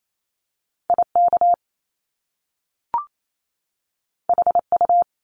Morse code, heard as a keyed tone, about 28 wpm, 700 Hz. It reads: IK HU
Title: Top Two Letter Combinations in Words 28wpm